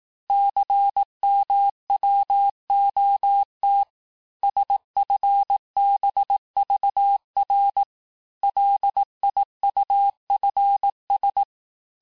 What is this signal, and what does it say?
Morse code, heard as a keyed tone, about 18 wpm, 800 Hz. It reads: CMWOT SFBVR LIUFS
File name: cours-du-lundi files / cours-1594--teste-chiffre-vitesse-17mpm